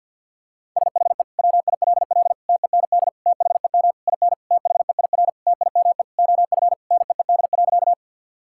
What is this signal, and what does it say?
Morse code, heard as a keyed tone, about 50 wpm, 700 Hz. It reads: SHE WILL TEND THEM IN THEIR TIME OF NEED.
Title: Sentences from Top 1000 Words - Review Part 5 - 50wpm